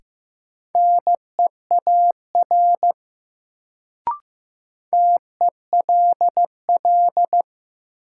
Morse code, heard as a keyed tone, about 15 wpm, 700 Hz. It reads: NEAR TELL
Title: Sentences from Top 200 Words - New Word Drill - 15wpm